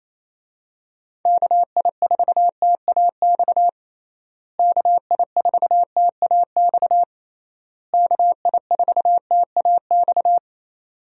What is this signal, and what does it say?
Morse code, heard as a keyed tone, about 28 wpm, 700 Hz. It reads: KI4TAX KI4TAX KI4TAX
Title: USA Call Signs - 6 Characters - Repeated 3x - 28wpm